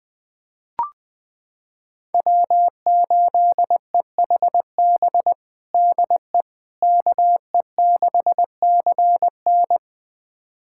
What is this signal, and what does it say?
Morse code, heard as a keyed tone, about 20 wpm, 700 Hz. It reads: W8EHB DE KE6CN
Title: USA Callsign DE Callsign - 5 Characters - 20wpm